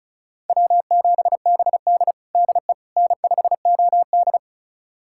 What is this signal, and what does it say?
Morse code, heard as a keyed tone, about 36 wpm, 700 Hz. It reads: W7BD DE N5OD
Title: USA Callsign DE Callsign - 4 Characters - 35wpm